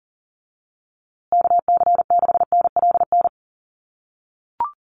Morse code, heard as a keyed tone, about 40 wpm, 700 Hz. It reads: KC6NLN
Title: USA Call Signs Repeated 3x 40wpm